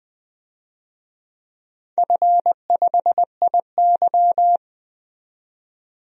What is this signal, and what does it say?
Morse code, heard as a keyed tone, about 20 wpm, 700 Hz. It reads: F5IY
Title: DXCC-100 Call Signs - 4 Characters - 20wpm